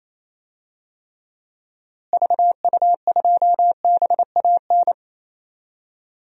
Morse code, heard as a keyed tone, about 28 wpm, 700 Hz. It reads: VU2BAN
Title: DXCC-100 Call Signs - 6 Characters - Repeated 3x - 28wpm